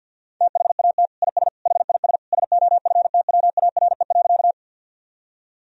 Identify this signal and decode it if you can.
Morse code, heard as a keyed tone, about 50 wpm, 700 Hz. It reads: THAT IS HIS SOFTWARE.
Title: Sentences from Top 1000 Words - Review Part 3 - 50wpm